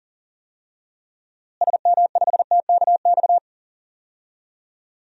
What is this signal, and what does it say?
Morse code, heard as a keyed tone, about 37 wpm, 700 Hz. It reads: SM5TKX
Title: DXCC-100 Call Signs - 6 Characters - Repeated 2x - 40wpm